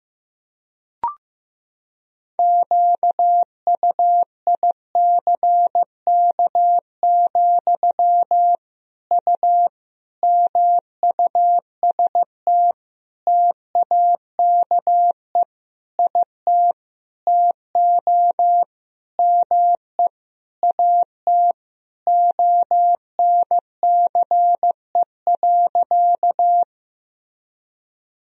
Morse code, heard as a keyed tone, about 15 wpm, 700 Hz. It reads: QUICK, U MUST TAKE IT TO ME AT ONCE.
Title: Sentences from Top 500 Words and Abbreviations 15wpm